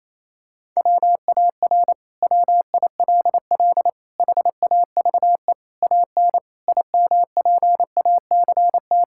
Morse code, heard as a keyed tone, about 28 wpm, 700 Hz. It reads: WAR WILL HAVE AN IMPACT
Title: Sentences from Top 1000 Words - Review Part 6 - 28wpm